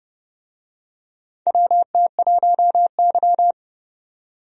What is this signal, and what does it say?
Morse code, heard as a keyed tone, about 32 wpm, 700 Hz. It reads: WT1Y
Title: USA Call Signs - 4 Characters - 30wpm